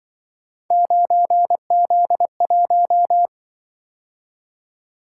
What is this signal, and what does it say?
Morse code, heard as a keyed tone, about 25 wpm, 700 Hz. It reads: 9Z1
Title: Speed-Racing - Three Letter-Number Combinations - 20wpm